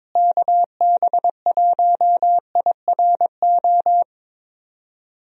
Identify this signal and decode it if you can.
Morse code, heard as a keyed tone, about 22 wpm, 700 Hz. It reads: KB1IRO
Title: USA Call Signs - 6 Characters - Repeated 3x - 22wpm